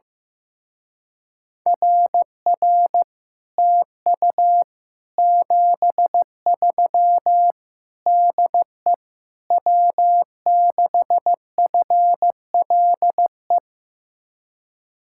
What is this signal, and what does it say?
Morse code, heard as a keyed tone, about 15 wpm, 700 Hz. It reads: RR TU 73 DE W6FLE
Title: Speed-Racing Common QSO Phrases Encore - 15wpm